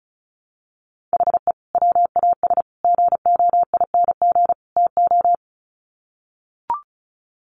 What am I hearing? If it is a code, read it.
Morse code, heard as a keyed tone, about 34 wpm, 700 Hz. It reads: HE WAS GOING TO
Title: Sets of 4 Words 35wpm